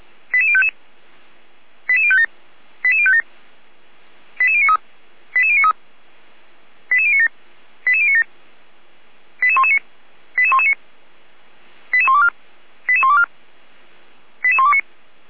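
Five-tone selective call (ZVEI), heard as quick runs of five tones, tones of about 70 ms. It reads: 88055 88057 88057 88093 88093 88097 88097 88119 88119 88124 88124 88129
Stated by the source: ZVEI five-tone selective call received by SDR, from feuerwehr-orlamuende.org fire brigade audio archive.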